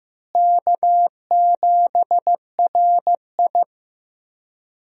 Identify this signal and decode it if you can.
Morse code, heard as a keyed tone, about 15 wpm, 700 Hz. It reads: K7RI
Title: USA Call Signs - 4 Characters - Repeated 3x - 15wpm